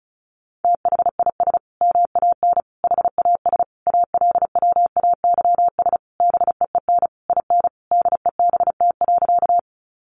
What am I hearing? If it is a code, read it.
Morse code, heard as a keyed tone, about 34 wpm, 700 Hz. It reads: THIS MAN HAS ALWAYS BEEN IN DEBT.
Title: Sentences from Top 1300 Words 35wpm